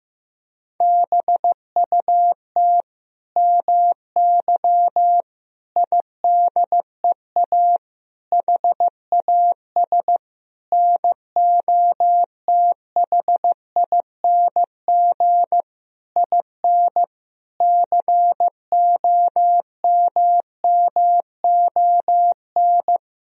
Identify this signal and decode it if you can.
Morse code, heard as a keyed tone, about 15 wpm, 700 Hz. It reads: BUT MY IDEA HAS NOTHING IN COMMON
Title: Sentences from Top 1000 Words - Review Part 3 - 15wpm